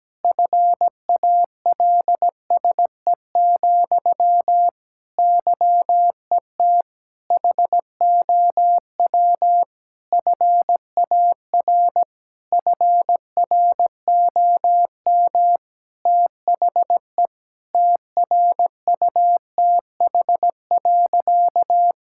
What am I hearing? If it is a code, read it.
Morse code, heard as a keyed tone, about 17 wpm, 700 Hz. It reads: FALSE, YET HOW FAR FROM THE TRUTH.
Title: Sentences from Top 1600 Words 17wpm